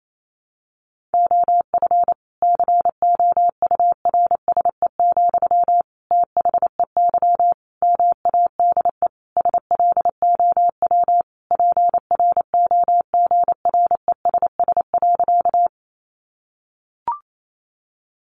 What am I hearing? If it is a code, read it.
Morse code, heard as a keyed tone, about 28 wpm, 700 Hz. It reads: OF COURSE, THEY MADE SLOW PROGRESS.
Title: Sentences from Top 1400 Words 28wpm